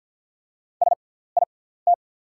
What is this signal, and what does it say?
Morse code, heard as a keyed tone, about 48 wpm, spaced as for 17 wpm, 690 Hz. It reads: SIT
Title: 6x Character Spacing - Top 500 Words - 50wpm